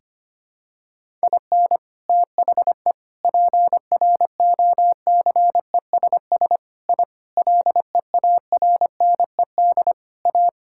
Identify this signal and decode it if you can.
Morse code, heard as a keyed tone, about 25 wpm, 700 Hz. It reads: IN THE PROCESS I LEARNED A
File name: Sentences with 3-Word Phrases - 25wpm